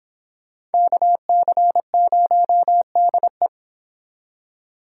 Morse code, heard as a keyed tone, about 28 wpm, 700 Hz. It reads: KC0DE